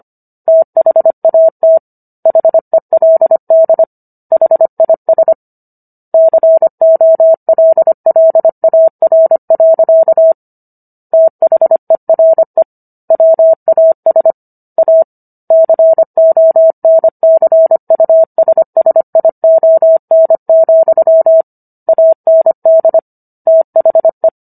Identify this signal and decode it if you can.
Morse code, heard as a keyed tone, about 25 wpm, 650 Hz. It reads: THAT HELD HIS COLLAR. THERE WAS A CONCUSSION, AND THE